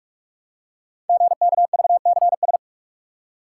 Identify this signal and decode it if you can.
Morse code, heard as a keyed tone, about 44 wpm, 700 Hz. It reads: GKVCS
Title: Code Groups - 5 Letters - 45wpm